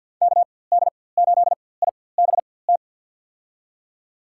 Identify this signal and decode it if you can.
Morse code, heard as a keyed tone, about 53 wpm, spaced as for 24 wpm, 700 Hz. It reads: KD8IBT
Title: USA Call Signs - 4x Character Spacing - 50wpm